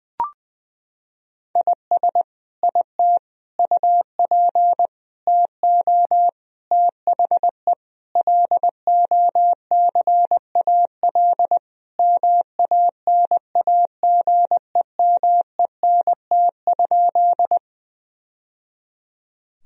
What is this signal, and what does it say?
Morse code, heard as a keyed tone, about 20 wpm, 700 Hz. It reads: IS IT UP TO THE LOCAL MANAGEMENT?
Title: Sentences from Top 1000 Words - Review Part 4 - 20wpm